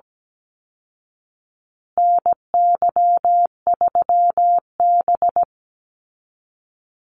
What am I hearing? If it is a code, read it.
Morse code, heard as a keyed tone, about 17 wpm, 700 Hz. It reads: NY3B
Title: Call Signs Repeated 3x - 17wpm